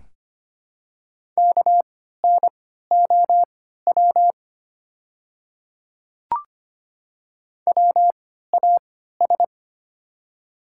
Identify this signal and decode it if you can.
Morse code, heard as a keyed tone, about 25 wpm, spaced as for 14 wpm, 700 Hz. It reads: KNOW WAS
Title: 3x Character Spacing - Top 200 Words - 25wpm